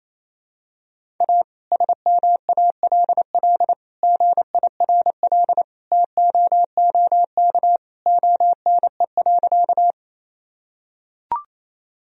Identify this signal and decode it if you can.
Morse code, heard as a keyed tone, about 27 wpm, 700 Hz. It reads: A SMALL GIRL TOOK ONE.
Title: Sentences from Top 1000 Words - Review Part 1 - 28wpm